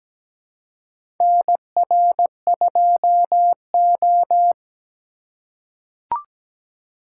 Morse code, heard as a keyed tone, about 17 wpm, 700 Hz. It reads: NR2O
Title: USA Call Signs - 4 Characters - 2x Character Spacing - 17wpm